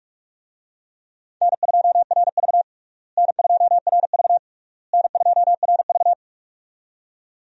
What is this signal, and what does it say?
Morse code, heard as a keyed tone, about 46 wpm, 700 Hz. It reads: N2RV N2RV N2RV
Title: USA Call Signs - 4 Characters - Repeated 3x - 45wpm